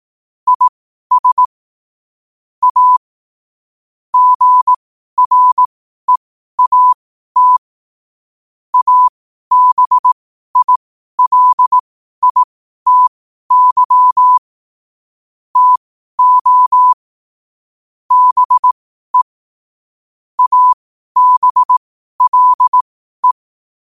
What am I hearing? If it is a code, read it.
Morse code, heard as a keyed tone, about 18 wpm, spaced as for 12 wpm, 1000 Hz. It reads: IS A GREAT ABILITY TO BE ABLE
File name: qotdfiles / 12wpm